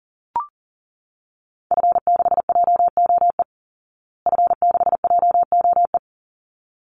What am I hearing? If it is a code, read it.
Morse code, heard as a keyed tone, about 40 wpm, 700 Hz. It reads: F6JOE F6JOE
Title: Call Signs Repeated 3x - 40wpm